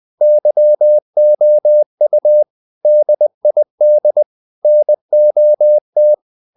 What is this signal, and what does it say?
Morse code, heard as a keyed tone, about 20 wpm, 600 Hz. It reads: YOU DID NOT
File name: the old man and the sea part 2 20wpm